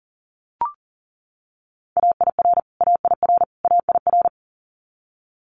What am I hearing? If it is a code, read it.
Morse code, heard as a keyed tone, about 40 wpm, 700 Hz. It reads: AIR AIR AIR
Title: Multi-letter Phonemes Repeated 3x 40wpm